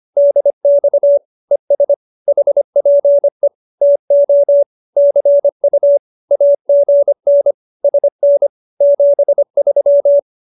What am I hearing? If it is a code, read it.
Morse code, heard as a keyed tone, about 25 wpm, 570 Hz. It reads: DX ES HPE TO CU AGN SN 73